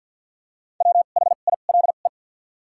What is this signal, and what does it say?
Morse code, heard as a keyed tone, about 53 wpm, spaced as for 37 wpm, 700 Hz. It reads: WHILE
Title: 2x Character Spacing - Top 300 Words - 50wpm